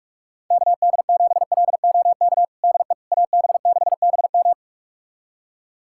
Morse code, heard as a keyed tone, about 46 wpm, 700 Hz. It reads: KD7LOK DE AB6BM
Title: USA Callsign DE Callsign 45wpm